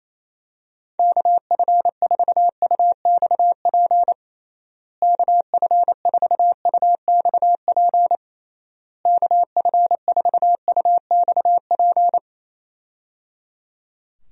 Morse code, heard as a keyed tone, about 28 wpm, 700 Hz. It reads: KF4UXP KF4UXP KF4UXP
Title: USA Call Signs - 6 Characters - Repeated 3x - 28wpm